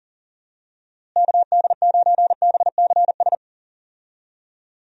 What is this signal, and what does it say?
Morse code, heard as a keyed tone, about 37 wpm, 700 Hz. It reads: KD9BCS
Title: USA Call Signs - 6 Characters - Repeated 3x - 40wpm